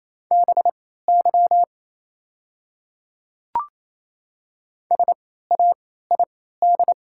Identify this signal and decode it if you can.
Morse code, heard as a keyed tone, about 28 wpm, spaced as for 16 wpm, 700 Hz. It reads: BY SAID
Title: 3x Character Spacing - Top 100 Words - 28wpm